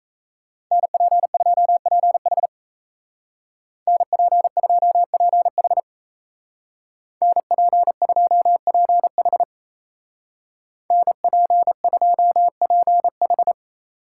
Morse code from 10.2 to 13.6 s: NP2PH

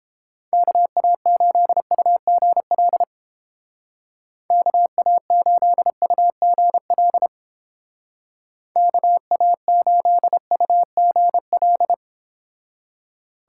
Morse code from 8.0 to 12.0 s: KA8UGL

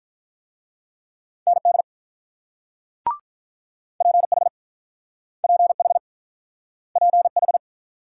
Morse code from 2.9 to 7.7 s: PH PH PH

